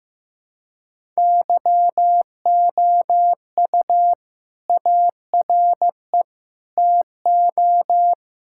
Morse code, heard as a keyed tone, about 15 wpm, 700 Hz. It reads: YOU ARE TO
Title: Sentences from Top 1600 Words 15wpm